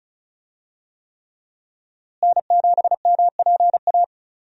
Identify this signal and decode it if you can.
Morse code, heard as a keyed tone, about 34 wpm, 700 Hz. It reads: N7MPA
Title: USA Call Signs - 5 Characters - Repeated 3x - 35wpm